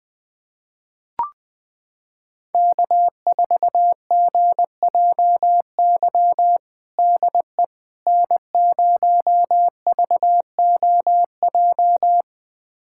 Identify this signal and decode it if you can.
Morse code, heard as a keyed tone, about 20 wpm, 700 Hz. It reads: K4GJY DE N0VOJ
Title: USA Callsign DE Callsign - 5 Characters - 20wpm